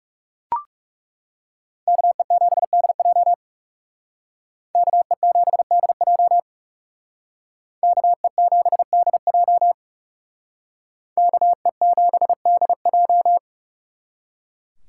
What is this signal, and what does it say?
Morse code, heard as a keyed tone, about 36 wpm, 700 Hz. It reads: KE7DJ KE7DJ KE7DJ KE7DJ
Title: Speed Racing - USA 2x2 Call Signs 30wpm